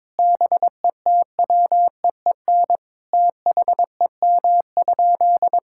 Morse code, heard as a keyed tone, about 21 wpm, 700 Hz. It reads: BETWEEN THEM?
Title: Sentences from Top 1000 Words - Review Part 6 - 22wpm